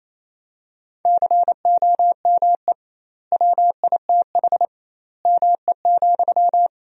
Morse code, heard as a keyed tone, about 28 wpm, 700 Hz. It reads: COME WITH ME,
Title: 2x Word-Spacing - Sentences from Top 200 Words - 28wpm